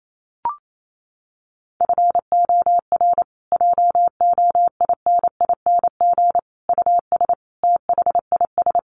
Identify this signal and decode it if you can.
Morse code, heard as a keyed tone, about 28 wpm, 700 Hz. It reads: FOR JOINING US THIS
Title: Sets of 5 Words Encore 28wpm